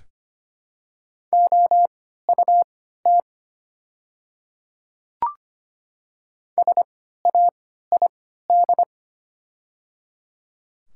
Morse code, heard as a keyed tone, about 25 wpm, spaced as for 14 wpm, 690 Hz. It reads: OUT SAID
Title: 3x Character Spacing - Top 100 Words - 25wpm